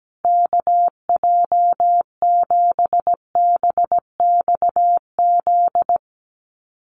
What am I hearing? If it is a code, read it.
Morse code, heard as a keyed tone, about 17 wpm, 700 Hz. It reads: KJ7BXZ
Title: USA Call Signs Repeated 3x 17wpm